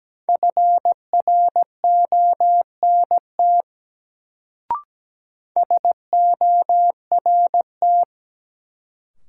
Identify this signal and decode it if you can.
Morse code, heard as a keyed tone, about 17 wpm, 700 Hz. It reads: FRONT SORT
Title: Sets of 2 Words - New Word Drill - 17wpm